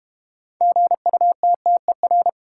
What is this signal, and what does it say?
Morse code, heard as a keyed tone, about 32 wpm, 700 Hz. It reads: GUTTER